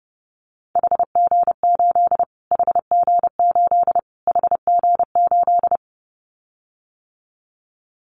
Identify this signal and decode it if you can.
Morse code, heard as a keyed tone, about 30 wpm, 700 Hz. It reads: HG8 HG8 HG8
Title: Callsign Prefixes for Top 25 DXCC - Repeated 3x - 30wpm